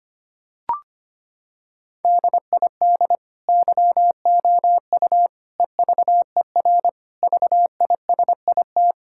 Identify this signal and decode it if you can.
Morse code, heard as a keyed tone, about 25 wpm, 700 Hz. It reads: DID YOU EVER VISIT
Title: Sentences from Top 1000 Words - Review Part 6 - 25wpm